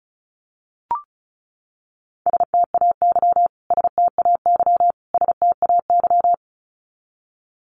Morse code, heard as a keyed tone, about 34 wpm, 700 Hz. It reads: STAY STAY STAY
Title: Top 500 Words Repeated 35wpm